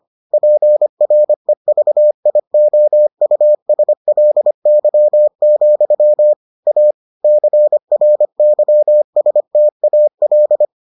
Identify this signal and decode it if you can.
Morse code, heard as a keyed tone, about 25 wpm, 600 Hz. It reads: PREVIOUSLY, A CRYSTAL